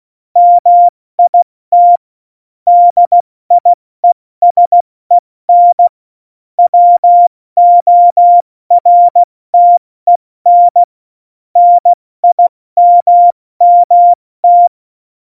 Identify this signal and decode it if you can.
Morse code, heard as a keyed tone, about 16 wpm, 700 Hz. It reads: MIT DIESEN WORTEN NIMMT
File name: Morsekurs / Witze / Witz 17. November 2020, 70 F80